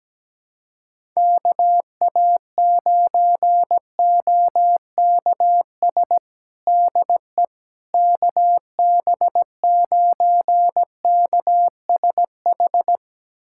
Morse code, heard as a keyed tone, about 17 wpm, 700 Hz. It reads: KA9OKS DE KB9KSH
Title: USA Callsign DE Callsign - 6 Characters - 17wpm